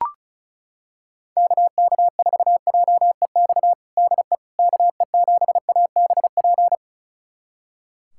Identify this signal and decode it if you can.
Morse code, heard as a keyed tone, about 34 wpm, 700 Hz. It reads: KK4JEX DE KE7ABP
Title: USA Callsign DE Callsign - 6 Characters - 35wpm